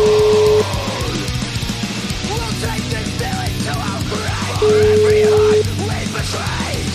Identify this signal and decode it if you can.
Ringback tone (Europe, 425 Hz), with noise.